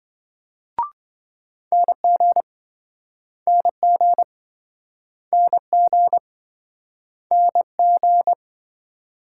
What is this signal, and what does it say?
Morse code, heard as a keyed tone, about 28 wpm, 700 Hz. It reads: NG NG NG NG